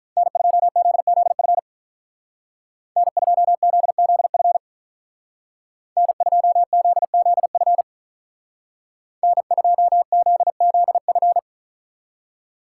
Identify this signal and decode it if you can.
Morse code, heard as a keyed tone, about 46 wpm, 700 Hz. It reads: N2ZZF N2ZZF N2ZZF N2ZZF